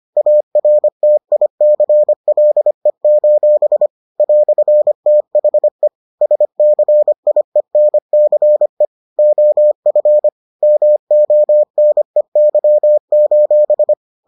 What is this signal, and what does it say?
Morse code, heard as a keyed tone, about 25 wpm, 600 Hz. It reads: ARTICLE: "THE SCIENCE OF MONEY: